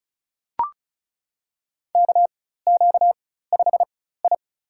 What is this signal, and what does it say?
Morse code, heard as a keyed tone, about 36 wpm, spaced as for 16 wpm, 700 Hz. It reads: KQ5I
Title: USA Call Signs - 4 Characters - 4x Character Spacing - 35wpm